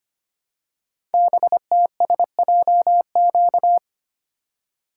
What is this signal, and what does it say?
Morse code, heard as a keyed tone, about 25 wpm, 700 Hz. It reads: BTSJQ
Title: Code Groups - 5 Letters - 25wpm